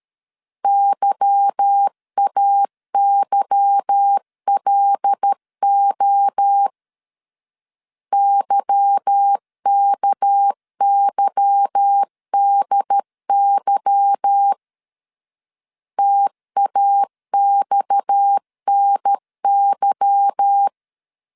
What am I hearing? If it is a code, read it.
Morse code, heard as a keyed tone, about 13 wpm, 800 Hz. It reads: YAYLO YKYDY TAXNY
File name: LEZIONI / 8 YJ45 / 8 Y 12wpm 2228